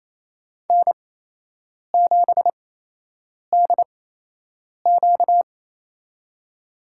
Morse code, heard as a keyed tone, about 28 wpm, spaced as for 8 wpm, 700 Hz. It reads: N7DQ
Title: USA Call Signs - 8x Character Spacing - 28wpm